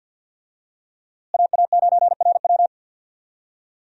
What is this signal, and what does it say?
Morse code, heard as a keyed tone, about 49 wpm, 700 Hz. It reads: AA9RW
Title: USA Call Signs - 5 Characters - Repeated 3x - 50wpm